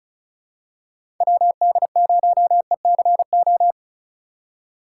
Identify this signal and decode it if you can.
Morse code, heard as a keyed tone, about 34 wpm, 700 Hz. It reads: WD0ECO
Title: K1USN SST QSO Practice - 35wpm